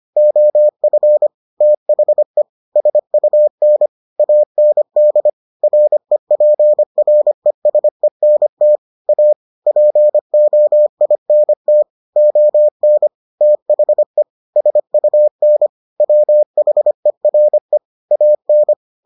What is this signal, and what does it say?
Morse code, heard as a keyed tone, about 25 wpm, 600 Hz. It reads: OF THE SUN AND REPRESENT A POINT ON THE SUN WHERE AN